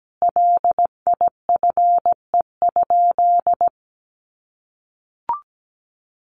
Morse code, heard as a keyed tone, about 17 wpm, 700 Hz. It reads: LIFE?
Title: Sentences from Top 1100 Words 17wpm